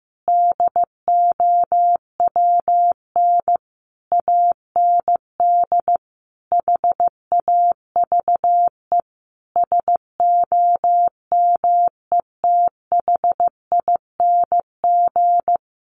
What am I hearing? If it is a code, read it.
Morse code, heard as a keyed tone, about 15 wpm, 700 Hz. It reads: DOWN AND HAVE SOMETHING
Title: Sentences from Top 600 Words 15wpm